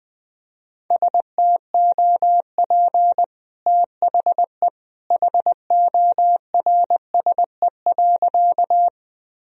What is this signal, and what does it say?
Morse code, heard as a keyed tone, about 20 wpm, 700 Hz. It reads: STOP THE HORSE.
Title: Sentences from Top 1000 Words - Review Part 1 - 20wpm